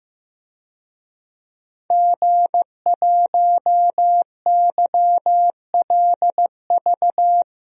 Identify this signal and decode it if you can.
Morse code, heard as a keyed tone, about 15 wpm, 700 Hz. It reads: G1YLV
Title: DXCC-100 Call Signs - 5 Characters - Repeated 3x - 15wpm